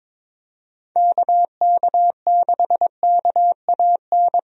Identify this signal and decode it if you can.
Morse code, heard as a keyed tone, about 22 wpm, 700 Hz. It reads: KK6KAN
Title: USA Call Signs - 6 Characters - Repeated 2x - 22wpm